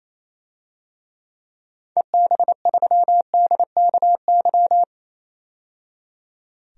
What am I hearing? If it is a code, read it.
Morse code, heard as a keyed tone, about 28 wpm, 700 Hz. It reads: EB3DKY
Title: DXCC-100 Call Signs - 6 Characters - 28wpm